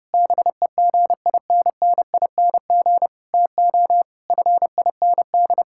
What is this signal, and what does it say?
Morse code, heard as a keyed tone, about 32 wpm, 700 Hz. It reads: BEGINNING TO FIND
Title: Sentences from Top 1000 Words - Review Part 5 - 30wpm